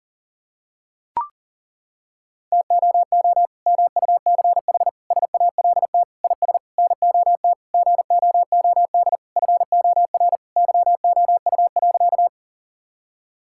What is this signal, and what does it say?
Morse code, heard as a keyed tone, about 40 wpm, 700 Hz. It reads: TOO MUCH SALT IS NOT GOOD FOR YOU.
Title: Sentences from Top 1000 Words - Review Part 5 - 40wpm